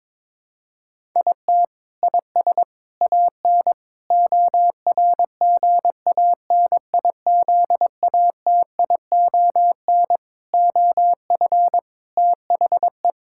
Morse code, heard as a keyed tone, about 22 wpm, 700 Hz. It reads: IT IS AN ORGANIZATION OF THE
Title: Sentences from Top 1000 Words - Review Part 6 - 22wpm